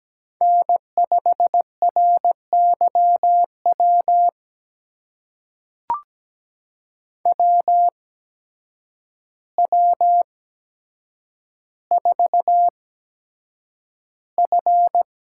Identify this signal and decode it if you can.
Morse code, heard as a keyed tone, about 17 wpm, spaced as for 8 wpm, 700 Hz. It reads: N5RYW WW4F
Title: USA Call Signs - 8x Character Spacing - 17wpm